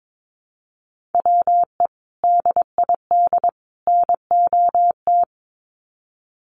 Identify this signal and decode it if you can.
Morse code, heard as a keyed tone, about 22 wpm, 700 Hz. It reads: WE DID NOT
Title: Sets of 3 Words 22wpm